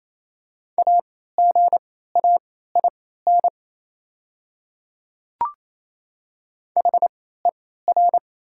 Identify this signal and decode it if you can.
Morse code, heard as a keyed tone, about 28 wpm, spaced as for 16 wpm, 690 Hz. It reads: AGAIN HER
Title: 3x Character Spacing - Top 500 Words - 28wpm